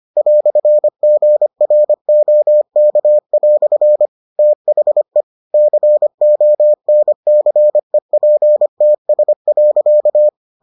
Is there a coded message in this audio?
Yes: Morse code from 0.2 to 10.3 s, heard as a keyed tone, about 25 wpm, 600 Hz. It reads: "GROK" THE CONCEPTS.